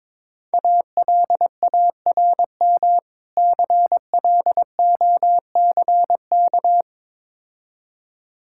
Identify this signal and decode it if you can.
Morse code, heard as a keyed tone, about 22 wpm, 700 Hz. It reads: ALARM CLOCK
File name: Easy Words - Speed-Racing with Hints - 22wpm